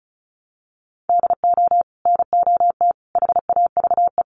Morse code, heard as a keyed tone, about 34 wpm, 700 Hz. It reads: DO NOT HAVE
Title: Sets of 4 Words 35wpm